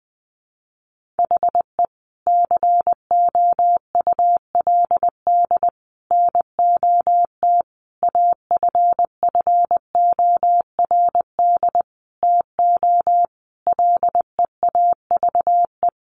Morse code, heard as a keyed tone, about 20 wpm, 700 Hz. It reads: HE COULD NOT AFFORD TO LEAVE
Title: Sentences from Top 1100 Words 20wpm